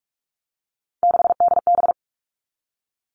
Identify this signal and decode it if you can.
Morse code, heard as a keyed tone, about 46 wpm, 700 Hz. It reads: 6DB
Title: Three Challenging Letter-Number Combinations 45wpm